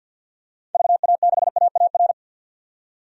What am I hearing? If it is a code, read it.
Morse code, heard as a keyed tone, about 52 wpm, 700 Hz. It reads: VA6AAR
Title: DXCC-100 Call Signs - 6 Characters - 50wpm